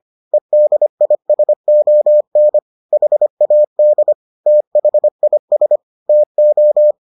Morse code, heard as a keyed tone, about 25 wpm, 600 Hz. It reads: EDISON HAD THIS TO